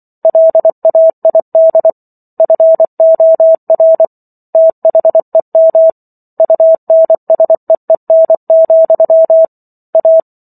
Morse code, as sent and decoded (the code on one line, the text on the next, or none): .-.. .- .. -.. / ..-. --- .-. / - .... -- / ..- -. ... . -. --..-- / .-
LAID FOR THEM UNSEEN, A